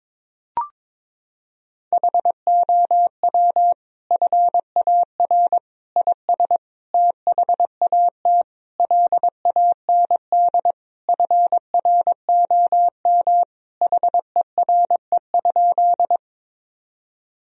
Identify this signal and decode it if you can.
Morse code, heard as a keyed tone, about 22 wpm, 700 Hz. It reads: HOW FAR IS THAT LAND FROM HERE?
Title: Sentences from Top 1000 Words - Review Part 1 - 22wpm